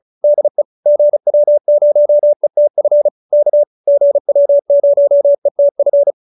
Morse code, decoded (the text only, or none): DE GW0ETF K GW0ETF